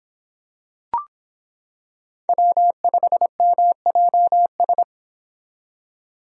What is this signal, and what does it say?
Morse code, heard as a keyed tone, about 26 wpm, 700 Hz. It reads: W5MJS